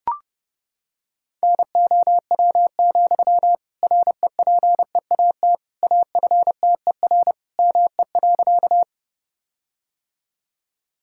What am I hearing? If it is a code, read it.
Morse code, heard as a keyed tone, about 30 wpm, 700 Hz. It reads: NOW, REPEAT AFTER ME.
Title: Sentences from Top 1600 Words 30wpm